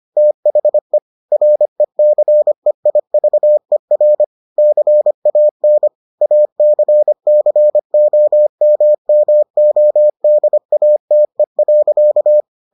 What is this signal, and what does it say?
Morse code, heard as a keyed tone, about 25 wpm, 600 Hz. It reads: THE RECEIVER CAN ACCOMMODATE.